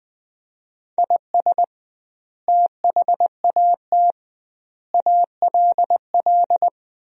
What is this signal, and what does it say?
Morse code, heard as a keyed tone, about 20 wpm, 700 Hz. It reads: IS THAT ALL
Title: 2x Word-Spacing - Sentences from Top 200 Words - 20wpm